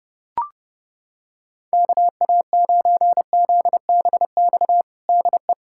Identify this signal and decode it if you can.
Morse code, heard as a keyed tone, about 32 wpm, 700 Hz. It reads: KA9ZBX DE
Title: USA Callsign DE Callsign - 6 Characters - 30wpm